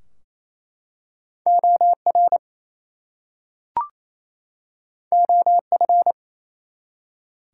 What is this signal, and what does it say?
Morse code, heard as a keyed tone, about 27 wpm, 700 Hz. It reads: OR OF
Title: Two Letter Words - 28wpm